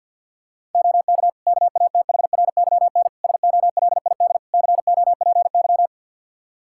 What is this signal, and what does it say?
Morse code, heard as a keyed tone, about 50 wpm, 700 Hz. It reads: OK KATHRYN SOLID COPY
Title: K1USN SST QSO Practice 50wpm